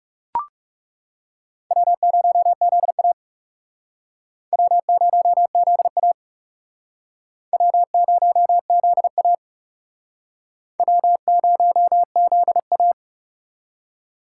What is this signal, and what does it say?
Morse code, heard as a keyed tone, about 40 wpm, 700 Hz. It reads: W0ZA W0ZA W0ZA W0ZA